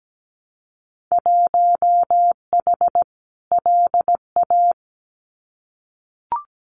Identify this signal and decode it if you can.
Morse code, heard as a keyed tone, about 17 wpm, 700 Hz. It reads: 1H LA
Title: Winter Field Day Exchanges 17wpm